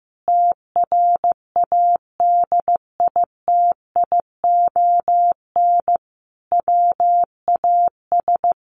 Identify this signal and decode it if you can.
Morse code, heard as a keyed tone, about 15 wpm, 700 Hz. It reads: TRADITION WAS
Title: Sentences from Top 1500 Words 15wpm